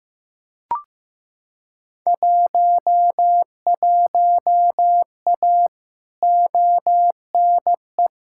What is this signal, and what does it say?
Morse code, heard as a keyed tone, about 15 wpm, 700 Hz. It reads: 11A ONE
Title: Field Day Exchanges Encore - 15wpm